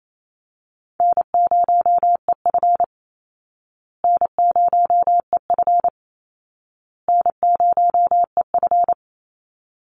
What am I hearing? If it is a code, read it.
Morse code, heard as a keyed tone, about 28 wpm, 700 Hz. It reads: N0EF N0EF N0EF
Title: Call Signs Repeated 2x - 28wpm